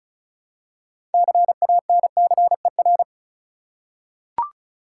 Morse code, heard as a keyed tone, about 34 wpm, 700 Hz. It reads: CANCER
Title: Six Letter Words - 35wpm